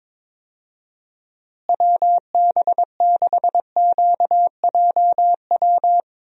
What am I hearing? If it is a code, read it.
Morse code, heard as a keyed tone, about 22 wpm, 700 Hz. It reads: WB6QJW
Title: USA Call Signs - 6 Characters - Repeated 3x - 22wpm